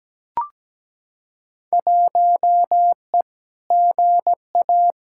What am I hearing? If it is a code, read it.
Morse code, heard as a keyed tone, about 17 wpm, 700 Hz. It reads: 1E GA
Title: Field Day Exchanges - 17wpm